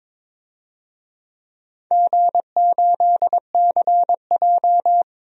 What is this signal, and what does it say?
Morse code, heard as a keyed tone, about 22 wpm, 700 Hz. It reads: G8CJ
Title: DXCC-100 Call Signs - 4 Characters - Repeated 3x - 22wpm